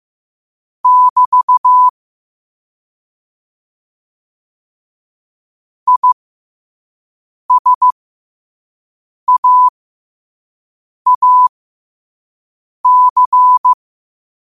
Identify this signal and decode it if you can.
Morse code, heard as a keyed tone, about 15 wpm, spaced as for 5 wpm, 1000 Hz. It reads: <BT> ISAAC
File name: qotdfiles / 5wpm